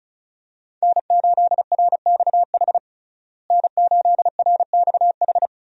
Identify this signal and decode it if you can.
Morse code, heard as a keyed tone, about 36 wpm, 700 Hz. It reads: N8RXH N8RXH
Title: USA Call Signs - 5 Characters - Repeated 3x - 35wpm